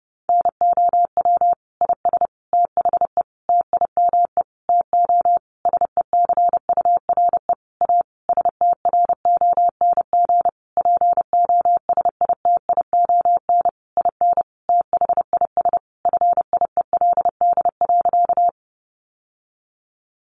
Morse code, heard as a keyed tone, about 30 wpm, 700 Hz. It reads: NOW IS THE TIME TO SECURE A STRONG POSITION IN THIS FIELD.
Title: Sentences from Top 1200 Words 30wpm